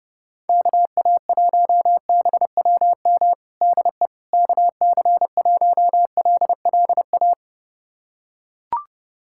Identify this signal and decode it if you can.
Morse code, heard as a keyed tone, about 30 wpm, 700 Hz. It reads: KA1BWM DE KC1LLA
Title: USA Callsign DE Callsign 30wpm